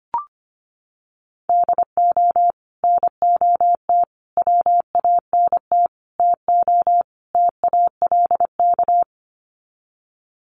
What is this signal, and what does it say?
Morse code, heard as a keyed tone, about 25 wpm, 700 Hz. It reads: DO NOT WANT TO TALK
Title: Sets of 5 Words 25wpm